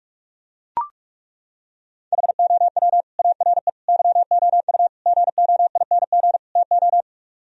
Morse code, heard as a keyed tone, about 38 wpm, 700 Hz. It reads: HOW ARE YOU GOING TO